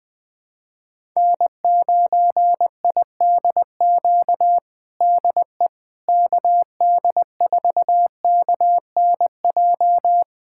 Morse code, heard as a keyed tone, about 21 wpm, 700 Hz. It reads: N9IDQ DE KD4KNJ
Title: USA Callsign DE Callsign 20wpm